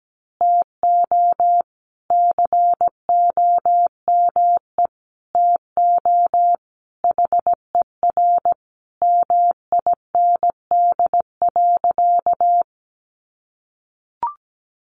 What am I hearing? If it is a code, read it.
Morse code, heard as a keyed tone, about 17 wpm, 700 Hz. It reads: TO COME TO HER MIND.
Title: Sentences from Top 1300 Words 17wpm